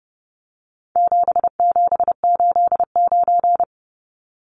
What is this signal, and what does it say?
Morse code, heard as a keyed tone, about 30 wpm, 700 Hz. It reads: 7789
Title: Numbers - 4-Digit 30wpm